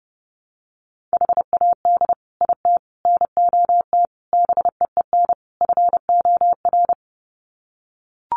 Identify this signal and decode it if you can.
Morse code, heard as a keyed tone, about 30 wpm, 700 Hz. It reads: HAD IT NOT BEEN FOR
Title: Sets of 5 Words Encore2 30wpm